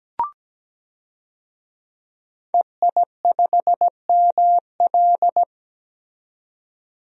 Morse code, heard as a keyed tone, about 17 wpm, 700 Hz. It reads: EI5ML
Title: DXCC-100 Call Signs - 5 Characters - 17wpm